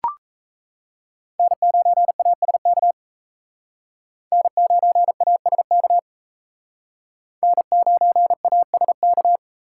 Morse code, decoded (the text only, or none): N9ASK N9ASK N9ASK